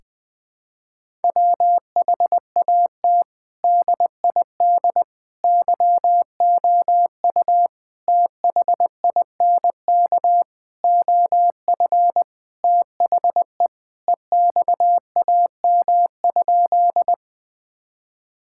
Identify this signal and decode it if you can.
Morse code, heard as a keyed tone, about 19 wpm, 700 Hz. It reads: WHAT DID YOU THINK OF THE EXAM?
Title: Sentences from Top 1000 Words - Review Part 5 - 20wpm